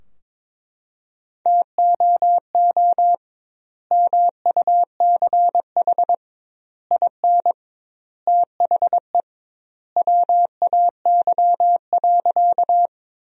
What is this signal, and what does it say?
Morse code, heard as a keyed tone, about 22 wpm, 700 Hz. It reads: TOO MUCH IN THE WAY.
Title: 2x Word-Spacing - Sentences from Top 200 Words - 22wpm